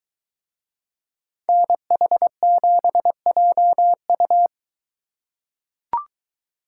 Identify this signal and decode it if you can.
Morse code, heard as a keyed tone, about 23 wpm, 700 Hz. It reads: NH7JU